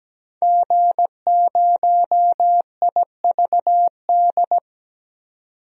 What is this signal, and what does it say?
Morse code, heard as a keyed tone, about 17 wpm, 700 Hz. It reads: G0IVD
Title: DXCC-100 Call Signs - 5 Characters - 17wpm